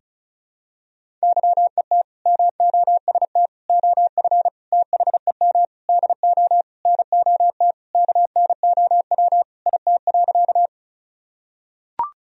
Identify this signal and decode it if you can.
Morse code, heard as a keyed tone, about 34 wpm, 700 Hz. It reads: YET MOST OF THEM DO NOT KNOW IT.
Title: Sentences from Top 1000 Words - Review Part 1 - 35wpm